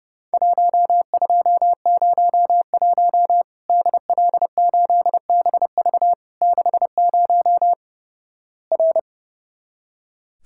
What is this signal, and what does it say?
Morse code, heard as a keyed tone, about 32 wpm, 700 Hz. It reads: 1201 DL8BV 60 R
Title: Worked All Europe DX Contest - QTC Exchange - 30wpm